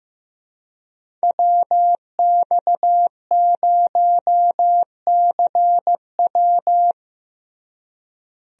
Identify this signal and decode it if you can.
Morse code, heard as a keyed tone, about 15 wpm, 700 Hz. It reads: WX0CW
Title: USA Call Signs - 5 Characters - Repeated 3x - 15wpm